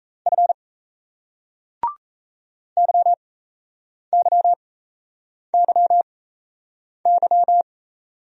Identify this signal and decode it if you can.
Morse code, heard as a keyed tone, about 35 wpm, spaced as for 8 wpm, 700 Hz. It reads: F YYYY
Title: Speed-Racing Single Letters 28wpm